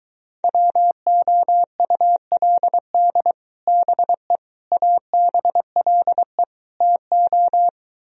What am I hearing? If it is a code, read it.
Morse code, heard as a keyed tone, about 23 wpm, 700 Hz. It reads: WOULD BE ABLE TO